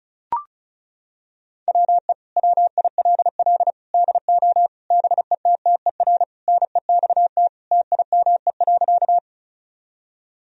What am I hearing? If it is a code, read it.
Morse code, heard as a keyed tone, about 34 wpm, 700 Hz. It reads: WE WILL DO BETTER NEXT TIME.
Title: Sentences from Top 1000 Words - Review Part 1 - 35wpm